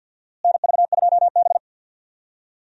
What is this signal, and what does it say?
Morse code, heard as a keyed tone, about 47 wpm, 700 Hz. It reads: NV2B
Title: USA Call Signs - 4 Characters - Repeated 2x - 50wpm